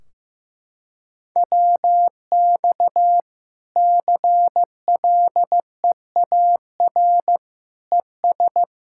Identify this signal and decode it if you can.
Morse code, heard as a keyed tone, about 15 wpm, 700 Hz. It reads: WX CLEAR ES
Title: Practice QSO - Exchange 2 - Info - 15wpm